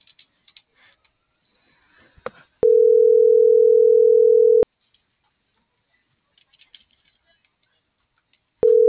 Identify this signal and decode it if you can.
Ringback tone (North America), 440 + 480 Hz.